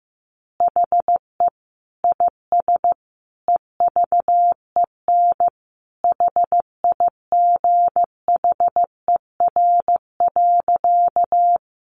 Morse code, heard as a keyed tone, about 15 wpm, 700 Hz. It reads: HE IS EVEN HIGHER.
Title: Additional Easy Phrases 15wpm